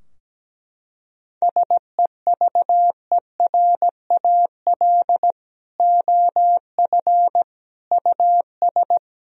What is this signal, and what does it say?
Morse code, heard as a keyed tone, about 17 wpm, 700 Hz. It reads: SEVERAL OF US
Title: Sentences from Top 1000 Words - Review Part 2 - 17wpm